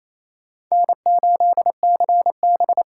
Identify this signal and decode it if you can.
Morse code, heard as a keyed tone, about 37 wpm, 700 Hz. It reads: N8CB